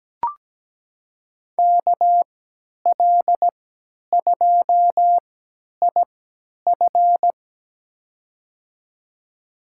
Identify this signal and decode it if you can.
Morse code, heard as a keyed tone, about 17 wpm, spaced as for 10 wpm, 700 Hz. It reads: KL2IF
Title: USA Call Signs - 5 Characters - 3x Character Spacing - 17wpm